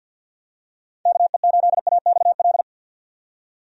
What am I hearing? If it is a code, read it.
Morse code, heard as a keyed tone, about 51 wpm, 700 Hz. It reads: KE8AXL
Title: USA Call Signs - 6 Characters - 50wpm